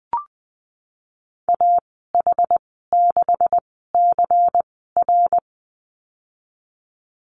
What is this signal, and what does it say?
Morse code, heard as a keyed tone, about 20 wpm, spaced as for 14 wpm, 700 Hz. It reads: AH6CR
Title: 186 Lesson 30g - Letter b - Call Signs 20wpm